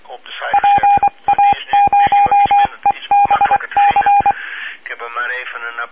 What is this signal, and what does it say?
Morse code, heard as a keyed tone, about 23 wpm, 810 Hz, with speech and radio noise. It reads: PA0EBC